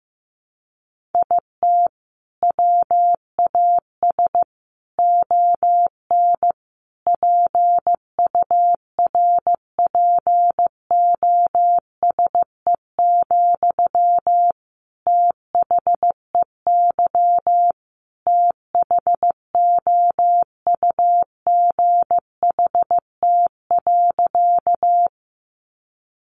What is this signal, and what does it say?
Morse code, heard as a keyed tone, about 15 wpm, 700 Hz. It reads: IT WAS ON PURPOSE, THEY THOUGHT.
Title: Sentences from Top 800 Words 15wpm